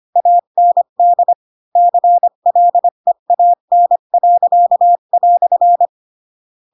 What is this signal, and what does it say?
Morse code, heard as a keyed tone, about 25 wpm, 700 Hz. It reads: AND CLEAN."
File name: Pinocchio / Pinocchio ch8 20 25 ik1hge